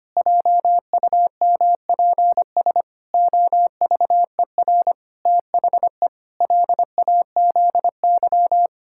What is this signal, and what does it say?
Morse code, heard as a keyed tone, about 25 wpm, 700 Hz. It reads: JUMPS OVER THE LAZY
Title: Pangram Challenge v2 - 25wpm